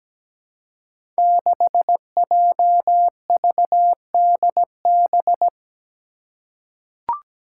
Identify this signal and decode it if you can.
Morse code, heard as a keyed tone, about 17 wpm, 700 Hz. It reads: 6JVDB
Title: Five Challenging Letter-Number Combinations 17wpm